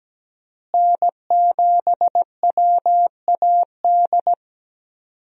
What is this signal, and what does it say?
Morse code, heard as a keyed tone, about 17 wpm, 700 Hz. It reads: N7WAD
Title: USA Call Signs - 5 Characters - Repeated 2x - 17wpm